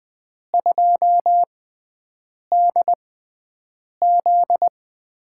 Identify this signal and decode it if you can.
Morse code, heard as a keyed tone, about 20 wpm, spaced as for 7 wpm, 700 Hz. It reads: 2DZ
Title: USA Call Signs - 4 Characters - 6x Character Spacing - 20wpm